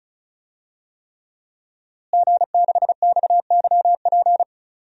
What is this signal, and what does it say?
Morse code, heard as a keyed tone, about 36 wpm, 700 Hz. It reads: G6XYP
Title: DXCC-100 Call Signs - 5 Characters - Repeated 3x - 35wpm